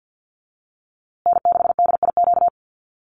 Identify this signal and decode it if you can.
Morse code, heard as a keyed tone, about 49 wpm, 700 Hz. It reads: N6DIQ
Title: USA Call Signs Repeated 3x 50wpm